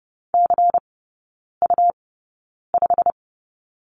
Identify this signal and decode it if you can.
Morse code, heard as a keyed tone, about 30 wpm, spaced as for 9 wpm, 700 Hz. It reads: CU5
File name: Lesson 17e - Letter u - Three Characters 30wpm